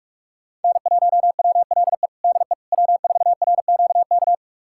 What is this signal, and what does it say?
Morse code, heard as a keyed tone, about 47 wpm, 700 Hz. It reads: N1WLE DE W4RQK
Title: USA Callsign DE Callsign - 5 Characters - 45wpm